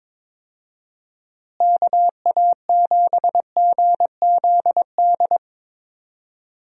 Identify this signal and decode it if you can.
Morse code, heard as a keyed tone, about 22 wpm, 700 Hz. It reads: KA7GZD
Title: USA Call Signs - 6 Characters - Repeated 2x - 22wpm